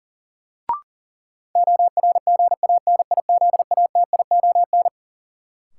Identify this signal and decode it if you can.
Morse code, heard as a keyed tone, about 40 wpm, 700 Hz. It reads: ORGANIZATION
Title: Sentences from Top 1000 Words - Review Part 5 - New Word Drill - 40wpm